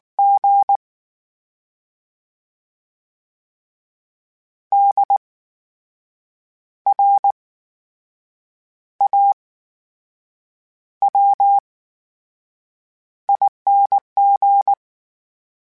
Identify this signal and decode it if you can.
Morse code, heard as a keyed tone, about 19 wpm, spaced as for 6 wpm, 800 Hz. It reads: G DRAWING
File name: Special INGs / Special INGs-mp3s / VerySlowToFaster Ing @18